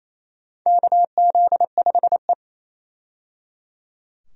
Morse code, heard as a keyed tone, about 28 wpm, 700 Hz. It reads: KZ5E
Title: USA Call Signs - 4 Characters - 28wpm